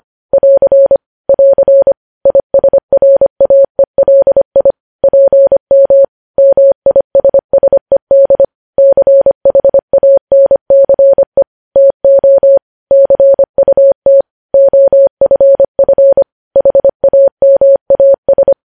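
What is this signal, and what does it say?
Morse code, heard as a keyed tone, about 25 wpm, 550 Hz. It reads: <AR> <AR> ISRAELI PM MISSED CHANCE TO CUT OFF HAMAS
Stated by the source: Morse code news headlines sent at 25, opening with vvv.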